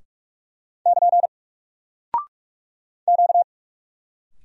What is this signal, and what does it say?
Morse code, heard as a keyed tone, about 44 wpm, 700 Hz. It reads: <KN> Q
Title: Single Character (Letter:Number:Punctuation:Prosign) 45wpm